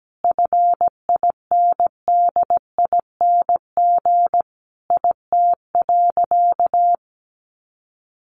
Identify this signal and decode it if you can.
Morse code, heard as a keyed tone, about 17 wpm, 700 Hz. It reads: FINDING IT.
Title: Sentences from Top 1100 Words 17wpm